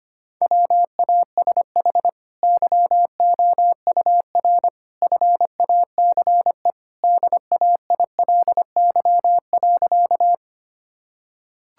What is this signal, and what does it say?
Morse code, heard as a keyed tone, about 25 wpm, 700 Hz. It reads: WASH YOUR FACE DAILY.
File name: Easy Sentences with Hints - 25wpm